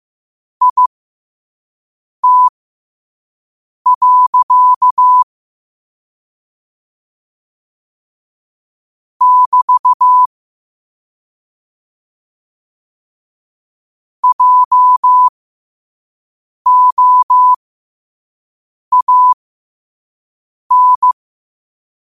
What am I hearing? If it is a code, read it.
Morse code, heard as a keyed tone, about 15 wpm, spaced as for 5 wpm, 1000 Hz. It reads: IT. <BT> JOAN
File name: qotdfiles / 5wpm